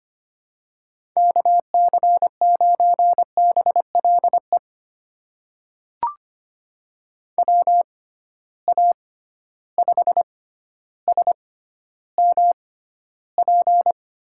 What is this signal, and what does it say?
Morse code, heard as a keyed tone, about 25 wpm, spaced as for 13 wpm, 700 Hz. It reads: KC9BLE WA5SMP
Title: USA Call Signs - 6 Characters - 6x Character Spacing - 25wpm